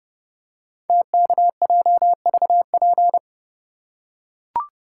Morse code, heard as a keyed tone, about 30 wpm, 700 Hz. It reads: TKJVP
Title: Code Groups - 5 Letters - 30wpm